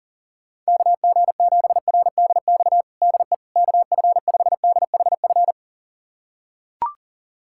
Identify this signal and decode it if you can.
Morse code, heard as a keyed tone, about 40 wpm, 700 Hz. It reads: KG7RDX DE KF5DHF
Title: USA Callsign DE Callsign - 6 Characters - 40wpm